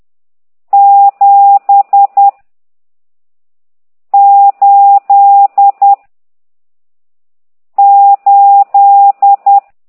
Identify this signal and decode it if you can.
Morse code, heard as a keyed tone, about 10 wpm, 800 Hz. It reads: 7 8 8
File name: ct2idw / exerc14